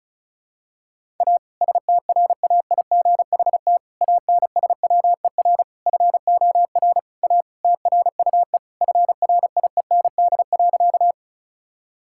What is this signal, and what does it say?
Morse code, heard as a keyed tone, about 34 wpm, 700 Hz. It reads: A STRAIGHT ANSWER FOR A TRUE FRIEND.
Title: Sentences from Top 1000 Words - Review Part 6 - 35wpm